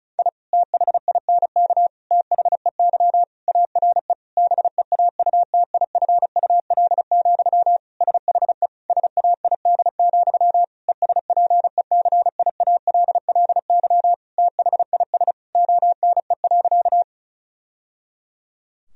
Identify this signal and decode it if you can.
Morse code, heard as a keyed tone, about 36 wpm, 700 Hz. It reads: I THINK THEY ARE BEAUTIFUL, SHE SAID, ESPECIALLY THIS ONE.
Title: Sentences from Top 1000 Words - Review Part 4 - 35wpm